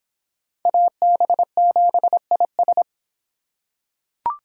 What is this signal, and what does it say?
Morse code, heard as a keyed tone, about 26 wpm, 700 Hz. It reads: AB7IS